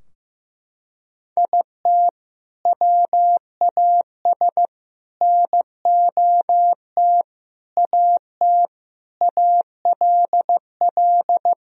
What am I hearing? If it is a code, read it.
Morse code, heard as a keyed tone, about 15 wpm, 700 Hz. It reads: IT WAS NOT AT ALL
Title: Sentences from Top 1000 Words - Review Part 4 - 15wpm